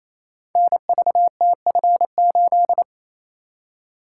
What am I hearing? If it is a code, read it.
Morse code, heard as a keyed tone, about 28 wpm, 690 Hz. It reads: NVTF8
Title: Code Groups - 5 Letter:Numbers - 28wpm